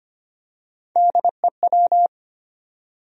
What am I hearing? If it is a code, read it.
Morse code, heard as a keyed tone, about 25 wpm, 700 Hz. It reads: DEW
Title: USA Call Sign Postfixes - 3 Letters 25wpm